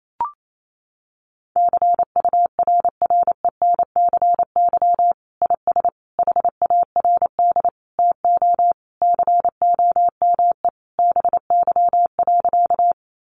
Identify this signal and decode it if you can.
Morse code, heard as a keyed tone, about 28 wpm, 700 Hz. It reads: CURRENCY IS HARD TO COME BY.
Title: Sentences from Top 1500 Words 28wpm